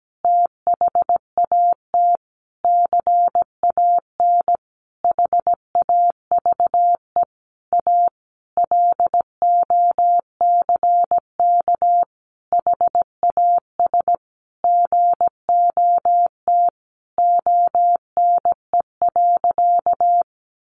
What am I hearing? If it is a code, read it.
Morse code, heard as a keyed tone, about 17 wpm, 700 Hz. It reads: THAT CAN HAVE A LOCK HAS GOT ONE.
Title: Sentences from Top 1300 Words 17wpm